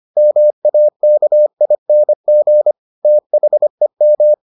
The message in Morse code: MAKING THEM

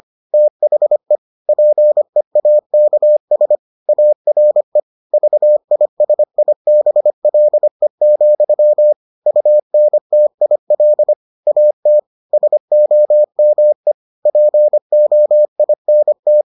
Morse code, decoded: THE PEAKS ARE VISIBLE, UNTIL AT SOME POINT

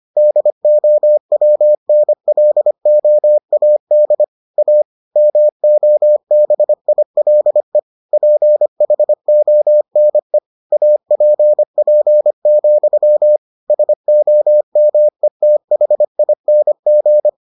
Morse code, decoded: DOWNLOAD A MOBILE PHONE APP, SOMETHING

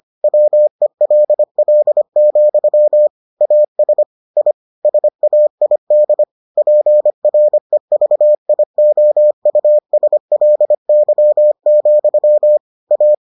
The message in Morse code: WELL, AS I SAID PREVIOUSLY, A

